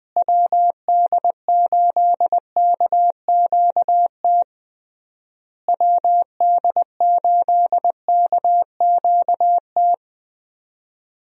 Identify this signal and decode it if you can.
Morse code, heard as a keyed tone, about 21 wpm, 700 Hz. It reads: WD8KQT WD8KQT
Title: USA Call Signs - 6 Characters - Repeated 3x - 20wpm